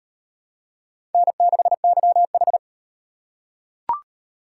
Morse code, heard as a keyed tone, about 37 wpm, 700 Hz. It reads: N6YH